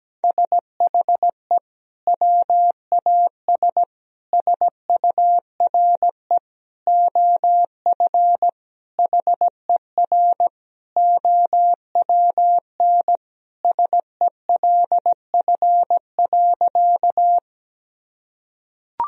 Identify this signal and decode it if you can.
Morse code, heard as a keyed tone, about 17 wpm, 700 Hz. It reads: SHE WAS SURE OF HER OWN SELF.
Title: Sentences from Top 1000 Words - Review Part 1 - 17wpm